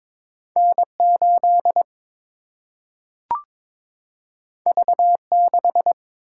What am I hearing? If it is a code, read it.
Morse code, heard as a keyed tone, about 22 wpm, 700 Hz. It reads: N 8 V 6
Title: Single-Letter Single-Number 22wpm